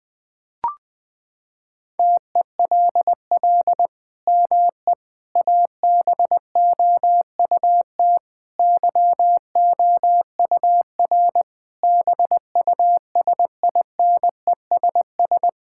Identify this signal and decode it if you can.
Morse code, heard as a keyed tone, about 20 wpm, 700 Hz. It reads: TELL ME ABOUT YOUR BUSINESS
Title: Sentences from Top 1000 Words - Review Part 5 - 20wpm